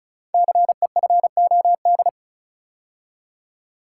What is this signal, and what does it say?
Morse code, heard as a keyed tone, about 36 wpm, 700 Hz. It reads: CEFOD
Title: Code Groups - 5 Letters - 35wpm